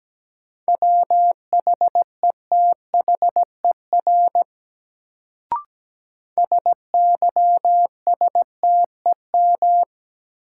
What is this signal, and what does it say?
Morse code, heard as a keyed tone, about 17 wpm, 700 Hz. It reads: WHETHER SYSTEM
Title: Sentences from Top 1000 Words - Review Part 3 - New Word Drill - 17wpm